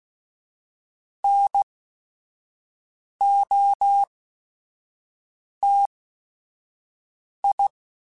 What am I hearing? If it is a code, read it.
Morse code, heard as a keyed tone, about 16 wpm, spaced as for 5 wpm, 800 Hz. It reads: NOTI
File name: QSOs / QSOs-mp3s / WD0BGP 1@5